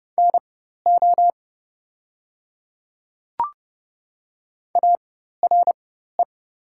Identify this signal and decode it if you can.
Morse code, heard as a keyed tone, about 30 wpm, spaced as for 14 wpm, 700 Hz. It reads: NO ARE
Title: 4x Character Spacing - Top 100 Words - 30wpm